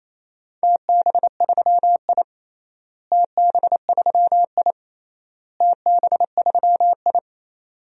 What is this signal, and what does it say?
Morse code, heard as a keyed tone, about 28 wpm, 700 Hz. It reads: TB3I TB3I TB3I
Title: DXCC-100 Call Signs - 4 Characters - Repeated 3x - 28wpm